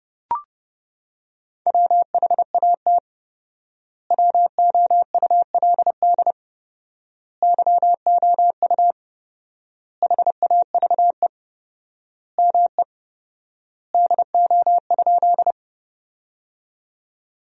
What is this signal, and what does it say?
Morse code, heard as a keyed tone, about 32 wpm, 700 Hz. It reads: WHAT WOULD YOU HAVE ME DO?
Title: 4x Word-Spacing - Sentences from Top 200 Words - 30wpm